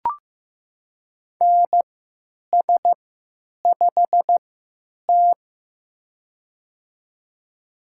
Morse code, heard as a keyed tone, about 15 wpm, spaced as for 9 wpm, 700 Hz. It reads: NS5T
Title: USA Call Signs - 4 Characters - 3x Character Spacing - 15wpm